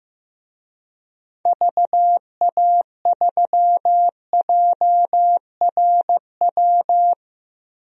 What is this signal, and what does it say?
Morse code, heard as a keyed tone, about 15 wpm, 700 Hz. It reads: VA3JRW
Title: DXCC-100 Call Signs - 6 Characters - 15wpm